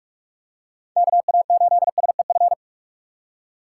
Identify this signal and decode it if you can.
Morse code, heard as a keyed tone, about 46 wpm, 700 Hz. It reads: KA8SEF
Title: USA Call Signs - 2x Character Spacing - 45wpm